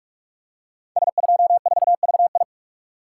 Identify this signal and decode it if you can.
Morse code, heard as a keyed tone, about 44 wpm, 700 Hz. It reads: S24VI
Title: Code Groups - 5 Letter:Numbers - 45wpm